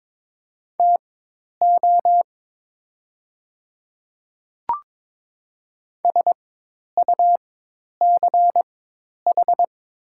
Morse code, heard as a keyed tone, about 22 wpm, spaced as for 10 wpm, 700 Hz. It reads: TO SUCH
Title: 4x Character Spacing - Top 200 Words - 22wpm